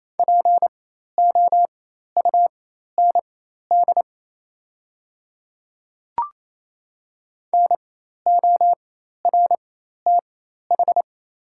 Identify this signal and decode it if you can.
Morse code, heard as a keyed tone, about 28 wpm, spaced as for 13 wpm, 700 Hz. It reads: POUND NORTH
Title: 4x Character Spacing - Top 500 Words - 28wpm